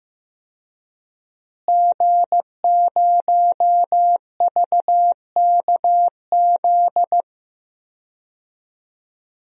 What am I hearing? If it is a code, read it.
Morse code, heard as a keyed tone, about 15 wpm, 700 Hz. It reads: G0VKZ
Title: DXCC-100 Call Signs - 5 Characters - Repeated 2x - 15wpm